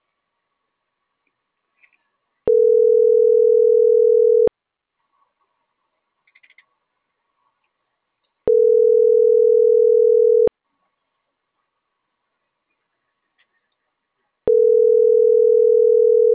Ringback tone (North America), 440 + 480 Hz.